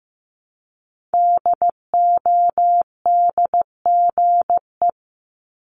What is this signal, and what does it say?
Morse code, heard as a keyed tone, about 15 wpm, 700 Hz. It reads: DODGE
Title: Multi-letter Phonemes Repeated 3x 15wpm